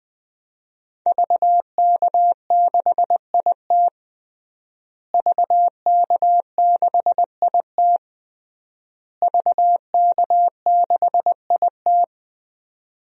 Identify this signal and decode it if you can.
Morse code, heard as a keyed tone, about 20 wpm, 700 Hz. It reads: VK6IT VK6IT VK6IT
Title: DXCC-100 Call Signs - 5 Characters - Repeated 3x - 20wpm